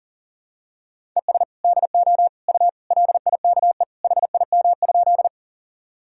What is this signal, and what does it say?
Morse code, heard as a keyed tone, about 40 wpm, 700 Hz. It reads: ES DO U LIKE HIM?
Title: Sentences from Top 100 Words and Abbreviations 40wpm